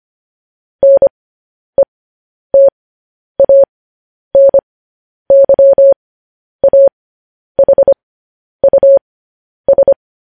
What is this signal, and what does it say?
Morse code, heard as a keyed tone, about 25 wpm, spaced as for 10 wpm, 550 Hz. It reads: NETANYAHUS
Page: news-headlines-in-morse-code-at-10-wpm